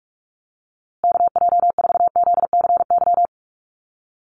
Morse code, heard as a keyed tone, about 44 wpm, 700 Hz. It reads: KJ4ZCY
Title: USA Call Signs Repeated 2x 45wpm